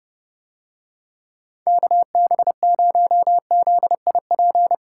Morse code, heard as a keyed tone, about 31 wpm, 700 Hz. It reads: KB0ZIP